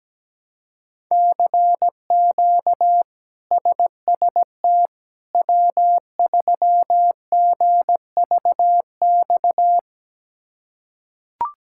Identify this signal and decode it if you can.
Morse code, heard as a keyed tone, about 17 wpm, 700 Hz. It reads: CQ SST W3GVX
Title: K1USN SST QSO Practice - 17wpm